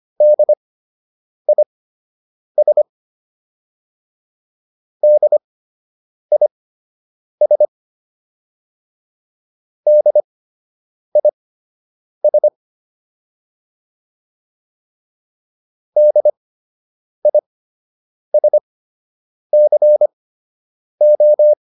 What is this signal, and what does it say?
Morse code, heard as a keyed tone, about 25 wpm, spaced as for 8 wpm, 600 Hz. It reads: DIS DIS DIS DISCO